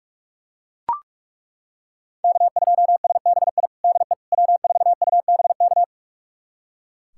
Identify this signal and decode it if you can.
Morse code, heard as a keyed tone, about 43 wpm, 700 Hz. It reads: K2SBI DE W4UBK
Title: USA Callsign DE Callsign - 5 Characters - 45wpm